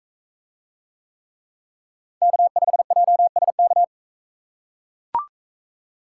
Morse code, heard as a keyed tone, about 40 wpm, 700 Hz. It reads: K5JSK